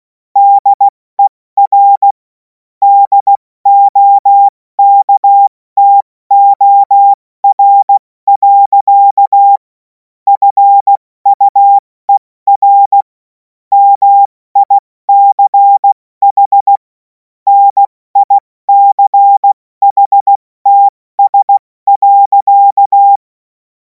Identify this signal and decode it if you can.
Morse code, heard as a keyed tone, about 16 wpm, 800 Hz. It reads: DER DOKTOR. FUER MICH NICHTS.
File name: Morsekurs / Witze / Witz 26. Mai 2026 70 F80